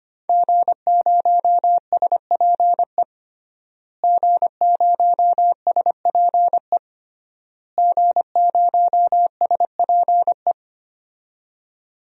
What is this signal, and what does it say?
Morse code, heard as a keyed tone, about 25 wpm, 700 Hz. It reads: G0SPE G0SPE G0SPE
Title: DXCC-100 Call Signs - 5 Characters - Repeated 3x - 25wpm